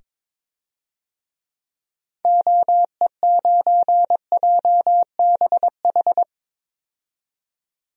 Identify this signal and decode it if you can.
Morse code, heard as a keyed tone, about 22 wpm, 700 Hz. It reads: OE9JBH
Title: DXCC-100 Call Signs - 6 Characters - 22wpm